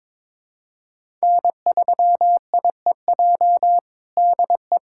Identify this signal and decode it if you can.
Morse code, heard as a keyed tone, about 22 wpm, 700 Hz. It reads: N3IEJ DE
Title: USA Callsign DE Callsign 22wpm